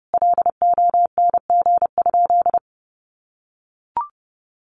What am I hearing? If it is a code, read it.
Morse code, heard as a keyed tone, about 30 wpm, 700 Hz. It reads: LONG?
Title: Sentences from Top 200 Words 30wpm